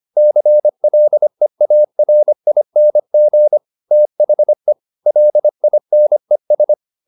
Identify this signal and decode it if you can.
Morse code, heard as a keyed tone, about 25 wpm, 600 Hz. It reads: CLEARING THE LINES
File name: the old man and the sea part 1 25wpm